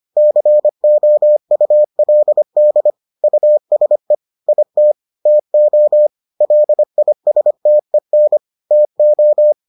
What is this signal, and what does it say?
Morse code, heard as a keyed tone, about 25 wpm, 600 Hz. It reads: COULD USE IT TO LISTEN TO